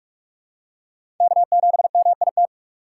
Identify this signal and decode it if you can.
Morse code, heard as a keyed tone, about 46 wpm, 700 Hz. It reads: K7MIT
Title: USA Call Signs - 5 Characters - 6x Character Spacing - 45wpm